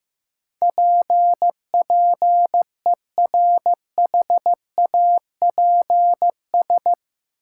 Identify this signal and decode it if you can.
Morse code, heard as a keyed tone, about 15 wpm, 700 Hz. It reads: PPERHAPS
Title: Sentences from Top 1100 Words - New Word Drill - 15wpm